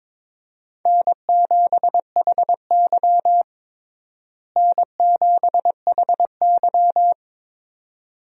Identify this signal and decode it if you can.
Morse code, heard as a keyed tone, about 22 wpm, 700 Hz. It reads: N7HY N7HY
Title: USA Call Signs - 4 Characters - Repeated 2x - 22wpm